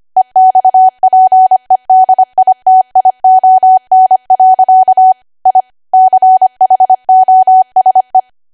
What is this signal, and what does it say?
Morse code, heard as a keyed tone, about 25 wpm, 750 Hz. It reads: EXPEDITION. I CHOSE
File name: Archive / 25 WPM / 25WPM